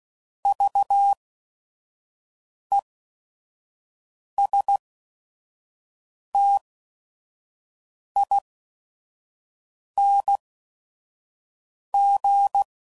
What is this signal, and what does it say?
Morse code, heard as a keyed tone, about 16 wpm, spaced as for 5 wpm, 800 Hz. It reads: VESTING
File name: Prefixes / By First Letter / Vs